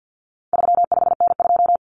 Morse code, heard as a keyed tone, about 48 wpm, 700 Hz. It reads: 35N2
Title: Common Cut Numbers - 4-Digit 50wpm